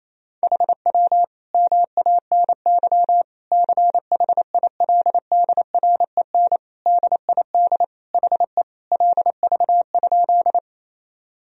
Morse code, heard as a keyed tone, about 28 wpm, 700 Hz. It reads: HW MANY CHILDREN DID HE LV?
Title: Sentences from Top 400 Words and Abbreviations 28wpm